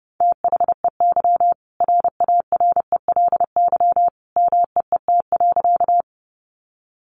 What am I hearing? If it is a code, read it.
Morse code, heard as a keyed tone, about 30 wpm, 700 Hz. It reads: THEY RARELY MEET.
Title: Sentences from Top 1400 Words 30wpm